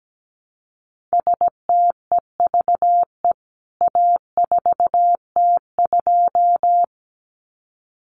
Morse code, heard as a keyed tone, about 17 wpm, 700 Hz. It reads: STEVE A4T2
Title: CWT Exchange 17wpm